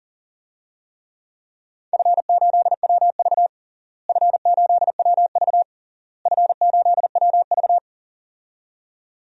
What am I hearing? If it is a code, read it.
Morse code, heard as a keyed tone, about 40 wpm, 700 Hz. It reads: F8WV F8WV F8WV
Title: DXCC-100 Call Signs - 4 Characters - Repeated 3x - 40wpm